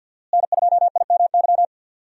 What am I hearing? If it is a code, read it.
Morse code, heard as a keyed tone, about 52 wpm, 700 Hz. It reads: N2IMY
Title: USA Call Signs - 5 Characters - Repeated 2x - 50wpm